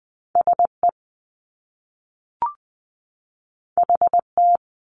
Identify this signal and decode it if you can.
Morse code, heard as a keyed tone, about 20 wpm, 700 Hz. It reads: SE HT
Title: Top Two Letter Combinations in Words 20wpm